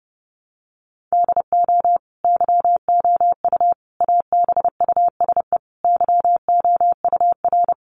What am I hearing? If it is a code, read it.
Morse code, heard as a keyed tone, about 30 wpm, 700 Hz. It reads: DO YOU ABUSE YOUR
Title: Sentences from Top 1300 Words 30wpm